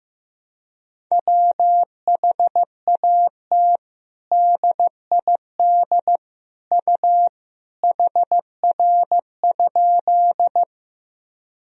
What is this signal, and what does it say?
Morse code, heard as a keyed tone, about 16 wpm, 700 Hz. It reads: WHAT DID U HR?
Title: Sentences from Top 300 Words and Abbreviations 15wpm